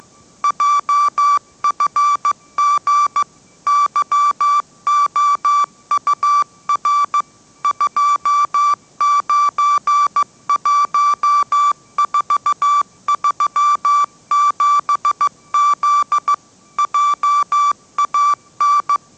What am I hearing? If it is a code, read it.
Morse code, heard as a keyed tone, about 15 wpm, 1160 Hz. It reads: JFG YOUR 291437Z JAN
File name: mrx / mrx065 16wpm